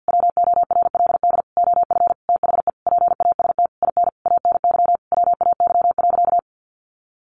Morse code, heard as a keyed tone, about 49 wpm, 700 Hz. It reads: WORLD OF THE PAST IN ANY WAY.
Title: A Sound Of Thunder 50wpm